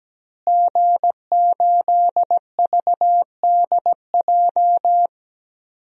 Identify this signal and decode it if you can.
Morse code, heard as a keyed tone, about 17 wpm, 700 Hz. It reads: G8VDJ
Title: DXCC-100 Call Signs - 5 Characters - Repeated 2x - 17wpm